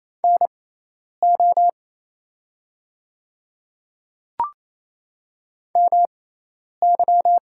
Morse code, heard as a keyed tone, about 28 wpm, spaced as for 10 wpm, 700 Hz. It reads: NO MY
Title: 6x Character Spacing - Top 100 Words - 28wpm